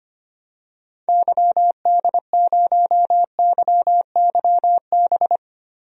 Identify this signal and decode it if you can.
Morse code, heard as a keyed tone, about 25 wpm, 700 Hz. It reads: YD0YYB
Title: DXCC-100 Call Signs - 6 Characters - Repeated 3x - 25wpm